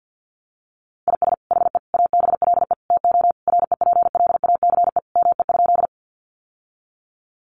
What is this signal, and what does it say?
Morse code, heard as a keyed tone, about 50 wpm, 700 Hz. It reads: IS HE ABLE TO REPLACE ME?
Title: Sentences from Top 1400 Words 50wpm